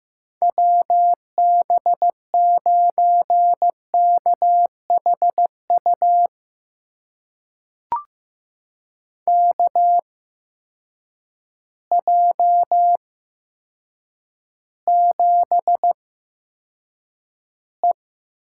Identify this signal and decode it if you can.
Morse code, heard as a keyed tone, about 15 wpm, spaced as for 8 wpm, 700 Hz. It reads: WB9KHU KJ7E
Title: USA Call Signs - 6 Characters - 8x Character Spacing - 15wpm